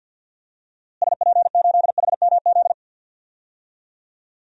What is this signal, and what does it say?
Morse code, heard as a keyed tone, about 52 wpm, 700 Hz. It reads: SP8HMZ
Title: DXCC-100 Call Signs - 6 Characters - Repeated 3x - 50wpm